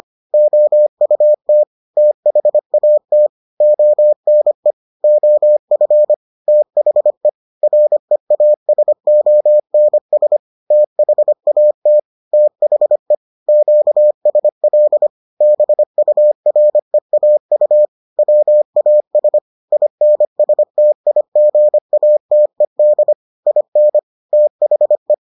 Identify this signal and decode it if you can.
Morse code, heard as a keyed tone, about 25 wpm, 600 Hz. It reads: OUT THAT ONE OF THE REASONS THAT THE QSL BUREAU WAS INSTIGATED IN THE